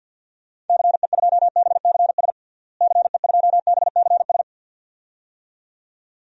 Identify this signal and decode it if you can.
Morse code, heard as a keyed tone, about 53 wpm, 700 Hz. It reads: CE2BCS CE2BCS
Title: DXCC-100 Call Signs - 6 Characters - Repeated 2x - 50wpm